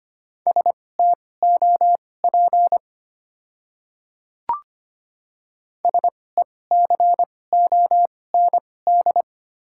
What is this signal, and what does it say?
Morse code, heard as a keyed tone, about 25 wpm, spaced as for 18 wpm, 690 Hz. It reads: STOP SECOND
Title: 2x Character Spacing - Top 300 Words - 25wpm